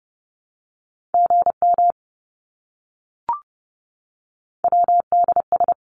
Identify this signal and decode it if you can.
Morse code, heard as a keyed tone, about 30 wpm, 700 Hz. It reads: GM WDS
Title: Abbreviations Repeated 3x - 30wpm